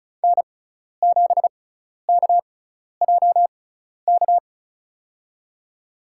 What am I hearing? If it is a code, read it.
Morse code, heard as a keyed tone, about 36 wpm, spaced as for 12 wpm, 700 Hz. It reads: N7KJK
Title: USA Call Signs - 6x Character Spacing - 35wpm